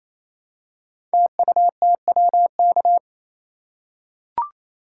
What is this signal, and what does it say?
Morse code, heard as a keyed tone, about 27 wpm, 700 Hz. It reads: TUTWK
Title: Code Groups - 5 Letters - 28wpm